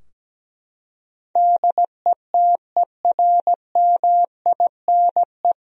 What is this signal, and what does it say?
Morse code, heard as a keyed tone, about 17 wpm, 700 Hz. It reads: DETERMINE
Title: Sentences from Top 1000 Words - Review Part 4 - 17wpm